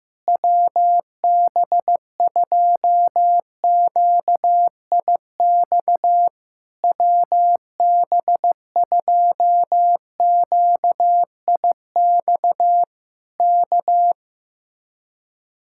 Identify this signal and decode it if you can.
Morse code, heard as a keyed tone, about 15 wpm, 700 Hz. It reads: WB2QIX WB2QIX K
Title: Speed-Racing Common QSO Phrases Encore - 15wpm